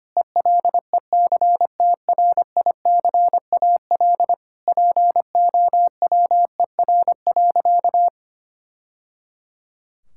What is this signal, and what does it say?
Morse code, heard as a keyed tone, about 25 wpm, 700 Hz. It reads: ELECTRICAL POWER.
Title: Sentences from Top 1000 Words - Review Part 6 - 25wpm